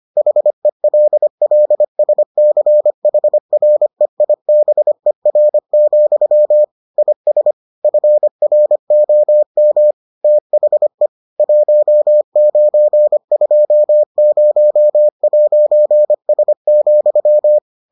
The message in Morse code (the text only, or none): HELLSCHREIBER, IS FROM THE 1920'S,